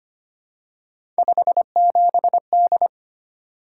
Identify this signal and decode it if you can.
Morse code, heard as a keyed tone, about 25 wpm, 700 Hz. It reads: 57D
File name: Three Letter-Number Combinations - 25wpm